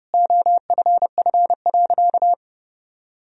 Morse code, heard as a keyed tone, about 30 wpm, 700 Hz. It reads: OFF.
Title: Sentences from Top 1000 Words - Review Part 1 - 30wpm